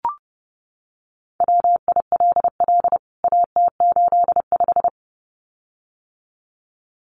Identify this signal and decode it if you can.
Morse code, heard as a keyed tone, about 30 wpm, 700 Hz. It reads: WILL AT85
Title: CWT Exchange 30wpm